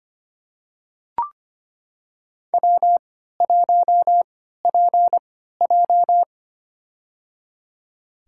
Morse code, heard as a keyed tone, about 25 wpm, spaced as for 14 wpm, 700 Hz. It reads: W1PJ
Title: USA Call Signs - 4 Characters - 3x Character Spacing - 25wpm